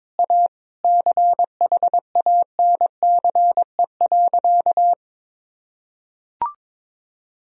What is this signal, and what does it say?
Morse code, heard as a keyed tone, about 21 wpm, 700 Hz. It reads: A CHANCE.
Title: Sentences from Top 1000 Words - Review Part 2 - 22wpm